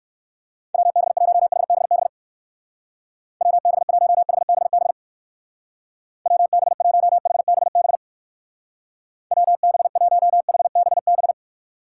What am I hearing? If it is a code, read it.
Morse code, heard as a keyed tone, about 59 wpm, 700 Hz. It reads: WB1HBB WB1HBB WB1HBB WB1HBB